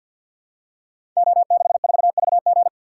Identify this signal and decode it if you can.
Morse code, heard as a keyed tone, about 52 wpm, 700 Hz. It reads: O64VG
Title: Code Groups - 5 Letter:Numbers - 50wpm